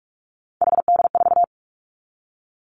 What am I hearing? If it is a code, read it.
Morse code, heard as a keyed tone, about 46 wpm, 700 Hz. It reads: HD4
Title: Three Challenging Letter-Number Combinations 45wpm